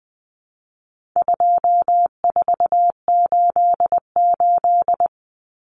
Morse code, heard as a keyed tone, about 20 wpm, 700 Hz. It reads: 2488
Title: Common Cut Numbers - 4-Digit 20wpm